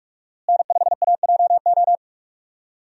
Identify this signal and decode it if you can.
Morse code, heard as a keyed tone, about 46 wpm, 700 Hz. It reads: N5AJO
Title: USA Call Signs - 5 Characters - Repeated 3x - 45wpm